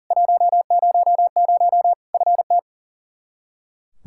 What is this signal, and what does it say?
Morse code, heard as a keyed tone, about 40 wpm, 700 Hz. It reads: 100 FT
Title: Practice QSO - Exchange 2 - Info - 40wpm